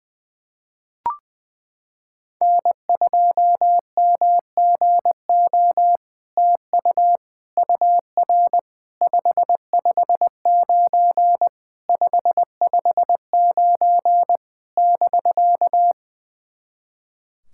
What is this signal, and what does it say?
Morse code, heard as a keyed tone, about 20 wpm, 700 Hz. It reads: N2MGO TU UR 559 559 <BK>
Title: SOTA QSO Practice 20wpm